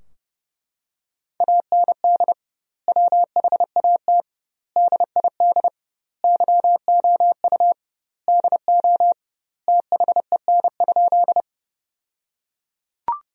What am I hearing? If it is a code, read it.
Morse code, heard as a keyed tone, about 30 wpm, 700 Hz. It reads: AND WHAT DID YOU DO THEN?
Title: 2x Word-Spacing - Sentences from Top 100 Words - 30wpm